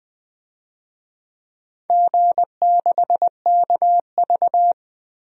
Morse code, heard as a keyed tone, about 20 wpm, 700 Hz. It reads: G6KV
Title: DXCC-100 Call Signs - 4 Characters - 20wpm